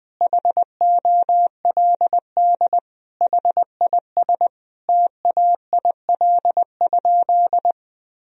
Morse code, heard as a keyed tone, about 20 wpm, 700 Hz. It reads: HOLD HIS TAIL?
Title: Sentences from Top 1000 Words - Review Part 3 - 20wpm